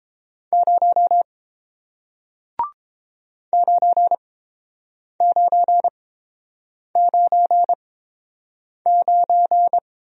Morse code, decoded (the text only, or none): 0 9999